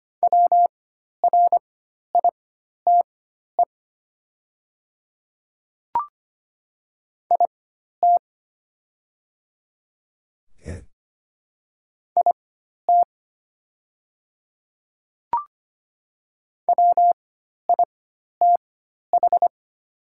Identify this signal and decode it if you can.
Morse code, heard as a keyed tone, about 25 wpm, spaced as for 12 wpm, 700 Hz. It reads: WRITE IT IT WITH
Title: 4x Character Spacing - Top 200 Words - 25wpm